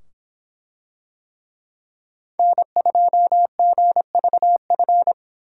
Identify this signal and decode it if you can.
Morse code, heard as a keyed tone, about 24 wpm, 700 Hz. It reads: N2GVF